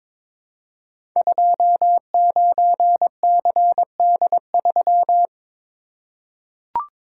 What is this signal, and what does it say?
Morse code, heard as a keyed tone, about 22 wpm, 700 Hz. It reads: 29CD3
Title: Code Groups - 5 Letter:Numbers - 22wpm